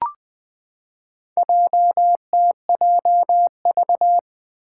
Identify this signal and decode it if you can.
Morse code, heard as a keyed tone, about 20 wpm, 700 Hz. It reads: JTJV
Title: Four Challenging Letter-Number Combinations 20wpm